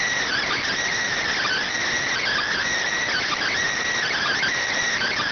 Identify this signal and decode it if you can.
Emergency Alert System (SAME) tones (header bursts), heard as harsh buzzing bursts. It is heard from 0.2 to 4.6 s.